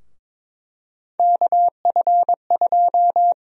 Morse code, heard as a keyed tone, about 21 wpm, 700 Hz. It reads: KF2
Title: USA Call Sign Prefixes - Repeated 2x - 22wpm